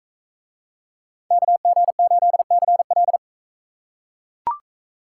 Morse code, heard as a keyed tone, about 40 wpm, 700 Hz. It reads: KG8CL